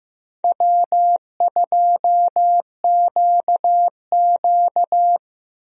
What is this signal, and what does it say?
Morse code, heard as a keyed tone, about 15 wpm, 700 Hz. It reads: W2QQ
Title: USA Call Signs - 4 Characters - Repeated 3x - 15wpm